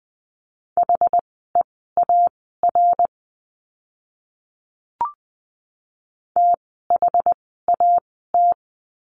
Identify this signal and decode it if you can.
Morse code, heard as a keyed tone, about 20 wpm, spaced as for 14 wpm, 700 Hz. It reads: HEAR THAT
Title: 244 Lesson 39f - Letter z - Words 20wpm